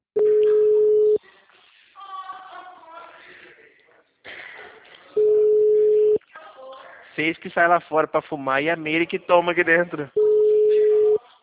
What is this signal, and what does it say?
Ringback tone (Europe, 425 Hz), with speech.